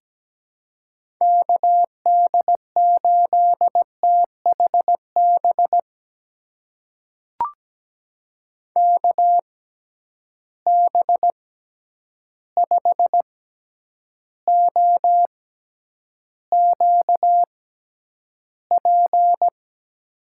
Morse code, heard as a keyed tone, about 17 wpm, spaced as for 9 wpm, 700 Hz. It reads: KD8THB KB5OQP
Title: USA Call Signs - 6 Characters - 6x Character Spacing - 17wpm